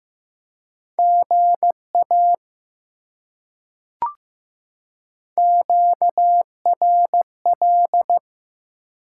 Morse code, heard as a keyed tone, about 15 wpm, 700 Hz. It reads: GA QRL
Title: Top 90 QSO Elements 15wpm